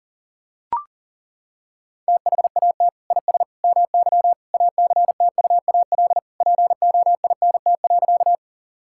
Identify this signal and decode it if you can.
Morse code, heard as a keyed tone, about 40 wpm, 700 Hz. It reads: THAT IS MY ACTUAL POINT.
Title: Sentences from Top 1000 Words - Review Part 5 - 40wpm